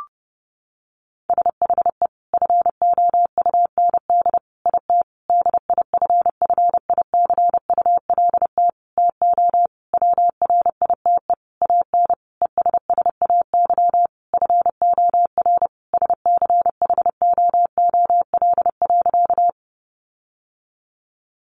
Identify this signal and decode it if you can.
Morse code, heard as a keyed tone, about 30 wpm, 700 Hz. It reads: SHE FOUND IT DIFFICULT TO WRITE AN ESSAY FOR SCHOOL.
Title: Sentences from Top 1500 Words 30wpm